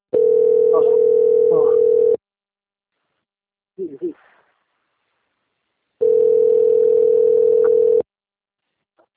Ringback tone (North America), 440 + 480 Hz.